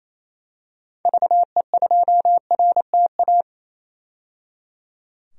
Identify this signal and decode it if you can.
Morse code, heard as a keyed tone, about 28 wpm, 700 Hz. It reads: VE2RTA
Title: DXCC-100 Call Signs - 6 Characters - 28wpm